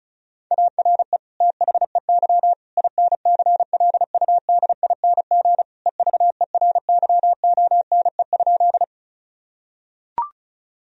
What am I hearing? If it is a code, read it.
Morse code, heard as a keyed tone, about 36 wpm, 700 Hz. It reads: ARE THEY INCLUDING EVERYONE?
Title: Sentences from Top 1000 Words - Review Part 4 - 35wpm